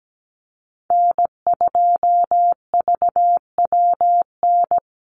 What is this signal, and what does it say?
Morse code, heard as a keyed tone, about 17 wpm, 700 Hz. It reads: N2VWN
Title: USA Call Signs Repeated 3x 17wpm